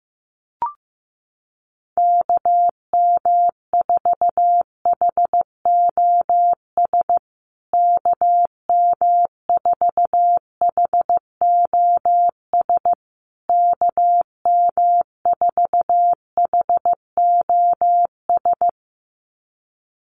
Morse code, heard as a keyed tone, about 15 wpm, 700 Hz. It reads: KM4HOS KM4HOS KM4HOS
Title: USA Call Signs Repeated 3x 15wpm